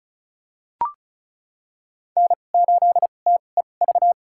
Morse code, heard as a keyed tone, about 34 wpm, spaced as for 25 wpm, 700 Hz. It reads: N8TEV
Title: USA Call Signs - 5 Characters - 2x Character Spacing - 35wpm